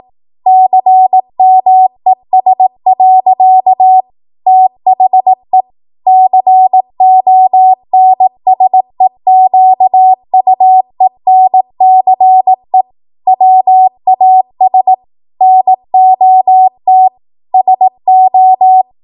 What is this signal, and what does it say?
Morse code, heard as a keyed tone, about 18 wpm, 750 Hz. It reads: CMES. THE CONSEQUENCE WAS NOT SO
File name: Morse / 18WPM